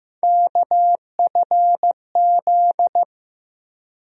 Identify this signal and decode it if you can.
Morse code, heard as a keyed tone, about 15 wpm, 700 Hz. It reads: KFZ
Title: USA Call Sign Postfixes - 3 Letters 15wpm